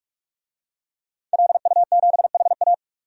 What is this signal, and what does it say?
Morse code, heard as a keyed tone, about 46 wpm, 700 Hz. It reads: LU7HA
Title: DXCC-100 Call Signs - 5 Characters - 45wpm